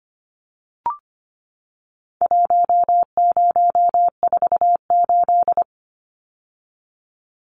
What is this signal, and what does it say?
Morse code, heard as a keyed tone, about 25 wpm, 700 Hz. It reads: 1048
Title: Numbers - Contest Style - 4-Digit 25wpm